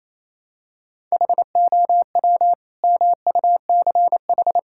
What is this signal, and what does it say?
Morse code, heard as a keyed tone, about 28 wpm, 700 Hz. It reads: HOW MUCH
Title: Sentences from Top 1000 Words - Review Part 6 - 28wpm